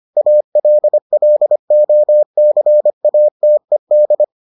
Morse code, heard as a keyed tone, about 25 wpm, 600 Hz. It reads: ALLOCATED